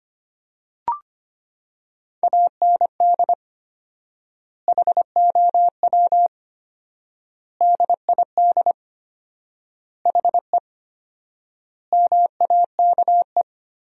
Morse code, heard as a keyed tone, about 25 wpm, 700 Hz. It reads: AND HOW DID HE MAKE
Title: 4x Word-Spacing - Sentences from Top 400 Words - 25wpm